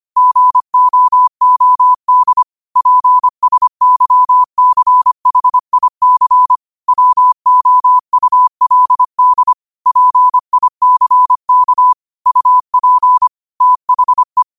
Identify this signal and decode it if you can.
Morse code, heard as a keyed tone, about 25 wpm, 1000 Hz. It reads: GOOD PSYCHIC WOULD PICK UP THE
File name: qotdfiles / 25wpm